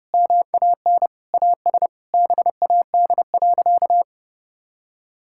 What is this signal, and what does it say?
Morse code, heard as a keyed tone, about 30 wpm, 700 Hz. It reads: MAN AS BAD.
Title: Sentences from Top 1000 Words - Review Part 6 - 30wpm